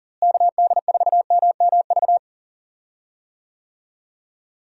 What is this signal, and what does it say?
Morse code, heard as a keyed tone, about 40 wpm, 700 Hz. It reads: KD4MMV
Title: USA Call Signs - 6 Characters - Repeated 3x - 40wpm